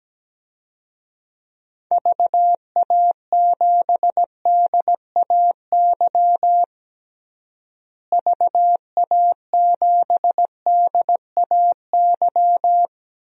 Morse code, heard as a keyed tone, about 17 wpm, 700 Hz. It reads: VA7DAY VA7DAY
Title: DXCC-100 Call Signs - 6 Characters - Repeated 3x - 17wpm